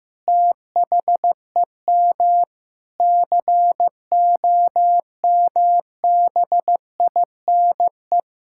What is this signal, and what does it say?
Morse code, heard as a keyed tone, about 15 wpm, 700 Hz. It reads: THEM COMBINE
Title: Sentences from Top 1600 Words 15wpm